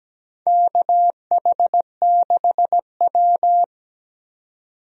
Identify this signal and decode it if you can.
Morse code, heard as a keyed tone, about 17 wpm, 700 Hz. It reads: KH6W
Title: DXCC-100 Call Signs - 4 Characters - 17wpm